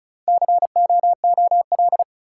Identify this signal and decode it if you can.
Morse code, heard as a keyed tone, about 34 wpm, 700 Hz. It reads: COOL
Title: Sentences from Top 1200 Words - New Word Drill - 35wpm